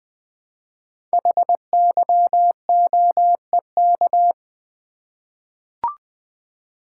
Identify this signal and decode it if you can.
Morse code, heard as a keyed tone, about 20 wpm, 700 Hz. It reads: HYOEK
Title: Code Groups - 5 Letters - 20wpm